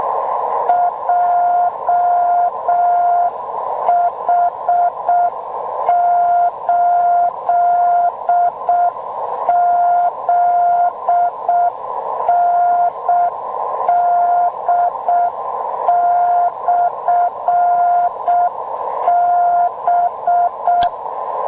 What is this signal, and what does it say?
Morse code, heard as a keyed tone, about 6 wpm, 700 Hz, with noise. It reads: JH8ZND/B